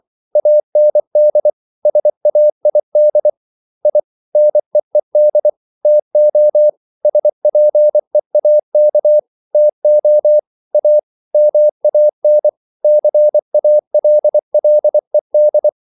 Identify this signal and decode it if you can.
Morse code, heard as a keyed tone, about 24 wpm, 600 Hz. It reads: AND SAID I NEED TO SPEAK TO A MAN CALLED